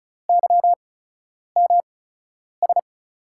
Morse code, heard as a keyed tone, about 34 wpm, spaced as for 10 wpm, 700 Hz. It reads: YMS
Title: USA Call Signs - 5 Characters - 8x Character Spacing - 35wpm